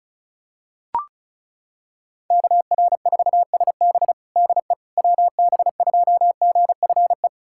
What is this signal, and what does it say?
Morse code, heard as a keyed tone, about 36 wpm, 700 Hz. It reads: KR4SB DE WB2GFE
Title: USA Callsign DE Callsign 35wpm